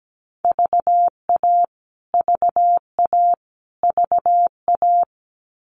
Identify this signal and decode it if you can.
Morse code, heard as a keyed tone, about 17 wpm, 700 Hz. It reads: VA VA VA
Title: US States Repeated 3x 17wpm